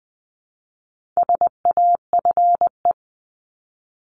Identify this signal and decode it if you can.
Morse code, heard as a keyed tone, about 20 wpm, 700 Hz. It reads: SAFE
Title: 260 Lesson 42e - Words 20-20wpm